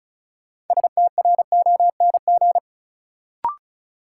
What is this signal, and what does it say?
Morse code, heard as a keyed tone, about 36 wpm, 700 Hz. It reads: STRONG
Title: Sentences from Top 500 Words - New Word Drill - 35wpm